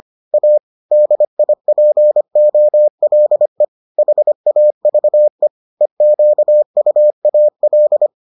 Morse code, heard as a keyed tone, about 25 wpm, 600 Hz. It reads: A DIPOLE HAVE EQUAL